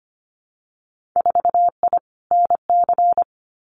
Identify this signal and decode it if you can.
Morse code, heard as a keyed tone, about 25 wpm, 700 Hz. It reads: 4I NC
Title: Winter Field Day Exchanges 25wpm